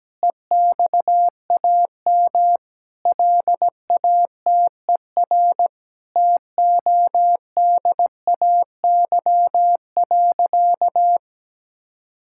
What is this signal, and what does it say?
Morse code, heard as a keyed tone, about 17 wpm, 700 Hz. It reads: EXAM LATER TODAY.
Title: Sentences from Top 1000 Words - Review Part 5 - 17wpm